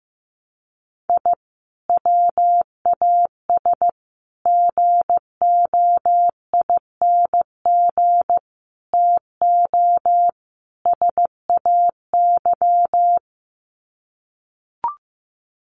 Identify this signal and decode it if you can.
Morse code, heard as a keyed tone, about 15 wpm, 700 Hz. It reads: I WAS GOING TO SAY
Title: Sets of 5 Words 15wpm